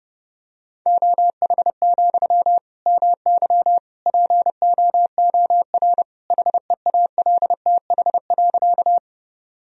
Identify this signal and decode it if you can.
Morse code, heard as a keyed tone, about 29 wpm, 700 Hz. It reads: OH, MY POOR HEALTH.
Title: Sentences from Top 1000 Words - Review Part 3 - 30wpm